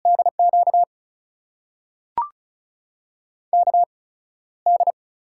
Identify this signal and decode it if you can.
Morse code, heard as a keyed tone, about 36 wpm, spaced as for 15 wpm, 700 Hz. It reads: DQ KD
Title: USA Call Signs - 6 Characters - 8x Character Spacing - 35wpm